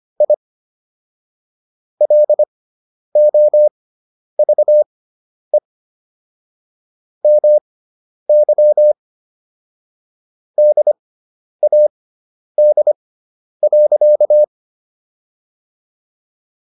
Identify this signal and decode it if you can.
Morse code, heard as a keyed tone, about 25 wpm, spaced as for 10 wpm, 600 Hz. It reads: I LOVE MY DAD.